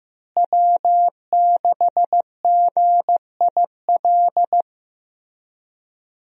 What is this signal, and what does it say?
Morse code, heard as a keyed tone, about 15 wpm, 700 Hz. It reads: W6GIL
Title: USA Call Signs - 5 Characters - Repeated 2x - 15wpm